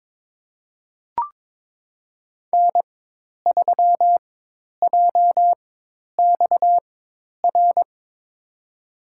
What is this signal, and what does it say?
Morse code, heard as a keyed tone, about 22 wpm, spaced as for 10 wpm, 700 Hz. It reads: N3JXR
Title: USA Call Signs - 5 Characters - 4x Character Spacing - 22wpm